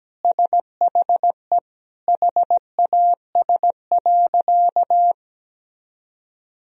Morse code, heard as a keyed tone, about 17 wpm, 700 Hz. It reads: SHE HAS.
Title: Sentences from Top 1000 Words - Review Part 2 - 17wpm